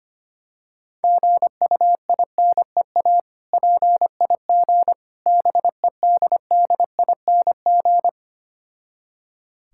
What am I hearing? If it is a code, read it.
Morse code, heard as a keyed tone, about 25 wpm, 700 Hz. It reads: GUINEA PIG BEDDING
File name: practice / 25wpm / Difficult Words with Hints - 25wpm